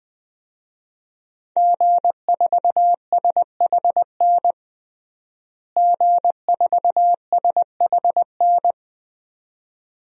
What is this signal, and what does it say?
Morse code, heard as a keyed tone, about 20 wpm, 700 Hz. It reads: G4SHN G4SHN
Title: DXCC-100 Call Signs - 5 Characters - Repeated 2x - 20wpm